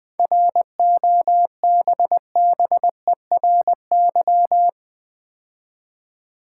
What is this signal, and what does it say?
Morse code, heard as a keyed tone, about 20 wpm, 690 Hz. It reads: ROBBERY
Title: 7 Letter Words - First 3 Letters Given - 20wpm